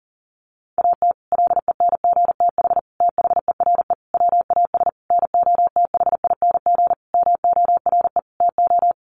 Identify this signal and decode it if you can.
Morse code, heard as a keyed tone, about 37 wpm, 700 Hz. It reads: AT LENGTH THERE WAS NOTHING MORE TO
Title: Sentences from Top 1000 Words 40wpm